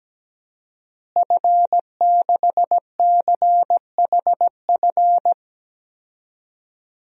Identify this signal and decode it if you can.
Morse code, heard as a keyed tone, about 17 wpm, 700 Hz. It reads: F6CHF
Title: DXCC-100 Call Signs - 5 Characters - Repeated 3x - 17wpm